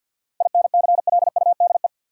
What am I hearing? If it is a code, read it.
Morse code, heard as a keyed tone, about 53 wpm, 700 Hz. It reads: INCLUDE
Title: Sentences from Top 1000 Words - Review Part 3 - New Word Drill - 50wpm